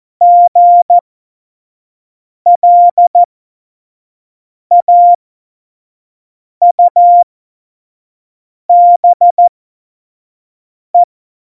Morse code, heard as a keyed tone, about 14 wpm, spaced as for 5 wpm, 700 Hz. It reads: GLAUBE
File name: Morsekurs / Witze / Witz 27 March 2018, 25 F70